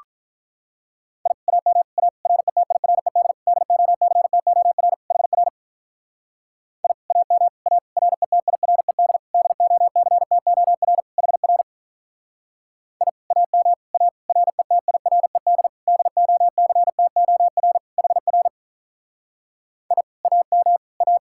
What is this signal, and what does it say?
Morse code, heard as a keyed tone, about 44 wpm, 700 Hz. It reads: I AM A RETIRED DOCTOR HR I AM A RETIRED DOCTOR HR I AM A RETIRED DOCTOR HR I AM A